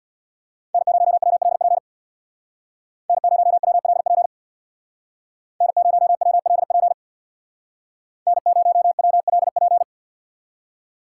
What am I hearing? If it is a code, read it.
Morse code, heard as a keyed tone, about 61 wpm, 700 Hz. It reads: N0WLP N0WLP N0WLP N0WLP
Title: Speed Racing - USA 1x3 Call Signs 50wpm